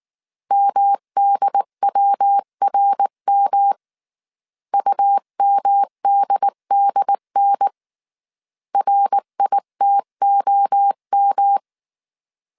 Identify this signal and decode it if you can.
Morse code, heard as a keyed tone, about 19 wpm, 800 Hz. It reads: MDWRM UMDDN RITOM
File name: CW / DET / 1 3 1500C S B 18wpm 1916